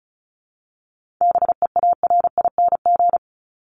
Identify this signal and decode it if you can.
Morse code, heard as a keyed tone, about 34 wpm, 700 Hz. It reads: BEARING
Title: ING Suffix - Top 100 Words with Base Words 35wpm